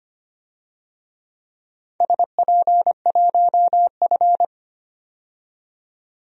Morse code, heard as a keyed tone, about 25 wpm, 700 Hz. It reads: SP1F
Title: DXCC-100 Call Signs - 4 Characters - 25wpm